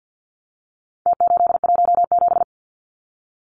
Morse code, heard as a keyed tone, about 48 wpm, 700 Hz. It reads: T817
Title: Numbers - Contest Style - 4-Digit 50wpm